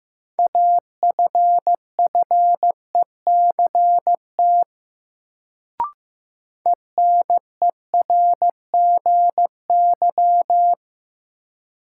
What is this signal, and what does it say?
Morse code, heard as a keyed tone, about 15 wpm, 700 Hz. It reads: AFFECT ENERGY
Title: Sentences from Top 900 Words - New Word Drill - 15wpm